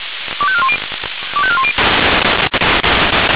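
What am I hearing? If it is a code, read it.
Five-tone selective call (ZVEI), heard as quick runs of five tones, tones of about 70 ms. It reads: 25410 25410
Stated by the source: ZVEI five-tone selective call received on an SDR.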